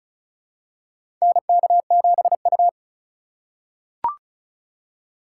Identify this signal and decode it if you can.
Morse code, heard as a keyed tone, about 34 wpm, 700 Hz. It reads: NK7U
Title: K1USN SST QSO Practice - 35wpm